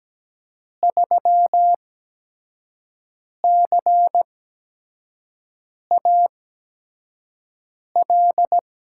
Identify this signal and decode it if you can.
Morse code, heard as a keyed tone, about 17 wpm, spaced as for 5 wpm, 700 Hz. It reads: 3CAL
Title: USA Call Signs - 5 Characters - 8x Character Spacing - 17wpm